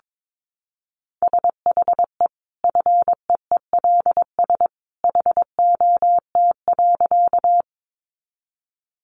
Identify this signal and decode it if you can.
Morse code, heard as a keyed tone, about 22 wpm, 700 Hz. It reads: SHE FEELS HOT.
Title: Additional Easy Phrases 22wpm